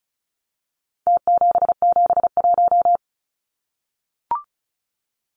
Morse code, heard as a keyed tone, about 34 wpm, 700 Hz. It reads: T771
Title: Numbers - Contest Style - 4-Digit 35wpm